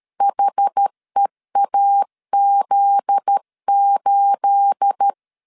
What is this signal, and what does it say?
Morse code, heard as a keyed tone, about 13 wpm, 800 Hz. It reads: HEAZ8
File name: LEZIONI / 9 28 12wpm 2351